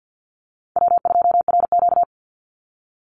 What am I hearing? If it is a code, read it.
Morse code, heard as a keyed tone, about 50 wpm, 700 Hz. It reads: W2RQ
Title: Call Signs Repeated 2x - 50wpm